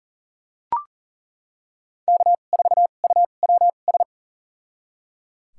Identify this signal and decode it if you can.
Morse code, heard as a keyed tone, about 37 wpm, spaced as for 28 wpm, 700 Hz. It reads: K4UWS
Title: USA Call Signs - 5 Characters - 2x Character Spacing - 40wpm